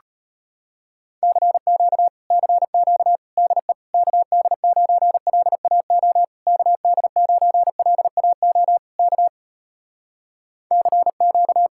Morse code, heard as a keyed tone, about 33 wpm, 700 Hz. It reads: CQ CQ DE KD9LAO KD9LAO K CQ